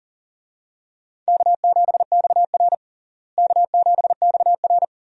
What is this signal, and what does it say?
Morse code, heard as a keyed tone, about 40 wpm, 700 Hz. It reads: K7XR K7XR
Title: USA Call Signs - 4 Characters - Repeated 2x - 40wpm